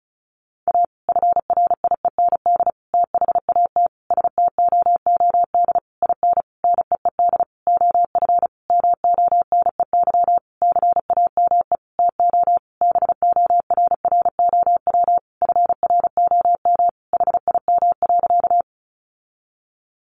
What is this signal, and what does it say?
Morse code, heard as a keyed tone, about 34 wpm, 700 Hz. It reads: A FRIEND THAT STOOD IN NEED OF MONEY CAME TO BORROW FROM HIM.
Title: Sentences from Top 1500 Words 35wpm